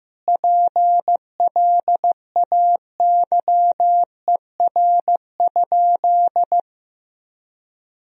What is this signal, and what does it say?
Morse code, heard as a keyed tone, about 16 wpm, 700 Hz. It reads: PLAYER?
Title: Sentences from Top 1000 Words - Review Part 4 - 15wpm